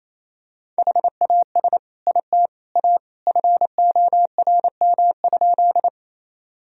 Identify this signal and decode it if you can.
Morse code, heard as a keyed tone, about 28 wpm, 700 Hz. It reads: HAS IT A FORM?
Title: Sentences from Top 1000 Words - Review Part 1 - 28wpm